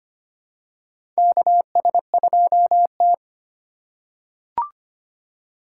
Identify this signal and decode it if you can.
Morse code, heard as a keyed tone, about 25 wpm, 700 Hz. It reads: KS2T
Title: USA Call Signs - 4 Characters - 2x Character Spacing - 25wpm